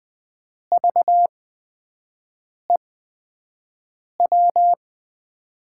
Morse code, heard as a keyed tone, about 20 wpm, spaced as for 5 wpm, 700 Hz. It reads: VEW
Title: USA Call Signs - 6 Characters - 8x Character Spacing - 20wpm